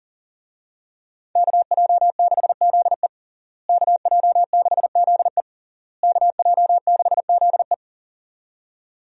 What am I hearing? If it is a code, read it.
Morse code, heard as a keyed tone, about 40 wpm, 700 Hz. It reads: KJ6ZE KJ6ZE KJ6ZE
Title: USA Call Signs - 5 Characters - Repeated 3x - 40wpm